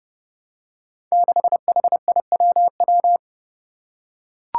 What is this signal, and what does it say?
Morse code, heard as a keyed tone, about 30 wpm, 700 Hz. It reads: 6HIWW
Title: Five Challenging Letter-Number Combinations 30wpm